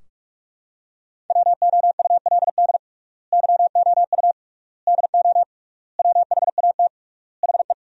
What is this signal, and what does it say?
Morse code, heard as a keyed tone, about 46 wpm, 700 Hz. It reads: WOULD YOU DO WHAT HE
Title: 3x Word-Spacing - Sentences from Top 100 Words - 45wpm